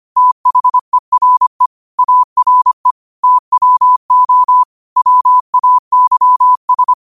Morse code, heard as a keyed tone, about 25 wpm, 1000 Hz. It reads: THERE ARE TWO WAYS
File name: qotdfiles / 25wpm